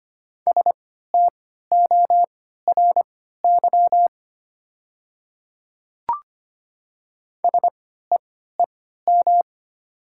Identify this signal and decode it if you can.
Morse code, heard as a keyed tone, about 25 wpm, spaced as for 14 wpm, 700 Hz. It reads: STORY SEEM
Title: 3x Character Spacing - Top 300 Words - 25wpm